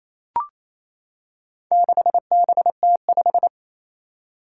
Morse code, heard as a keyed tone, about 28 wpm, 700 Hz. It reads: 6BT5
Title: Four Challenging Letter-Number Combinations 28wpm